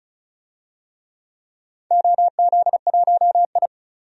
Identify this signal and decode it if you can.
Morse code, heard as a keyed tone, about 36 wpm, 690 Hz. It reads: OZ1I
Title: DXCC-100 Call Signs - 4 Characters - 35wpm